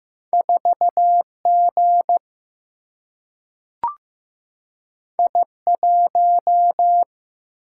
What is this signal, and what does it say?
Morse code, heard as a keyed tone, about 15 wpm, 700 Hz. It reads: 4G I1
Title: Single-Letter Single-Number 15wpm